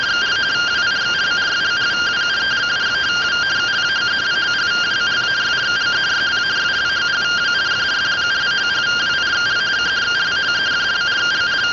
Two-tone teleprinter signal, about 170 Hz shift.
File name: findings / axial iodine 3mm 205.gen-43 rtty rx